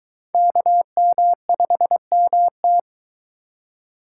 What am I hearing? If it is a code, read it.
Morse code, heard as a keyed tone, about 21 wpm, 700 Hz. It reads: KM5MT